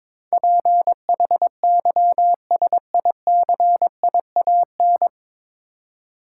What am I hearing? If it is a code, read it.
Morse code, heard as a keyed tone, about 22 wpm, 700 Hz. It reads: PHYSICIAN
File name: 9 Letter Words - First 3 Letters Given - 22wpm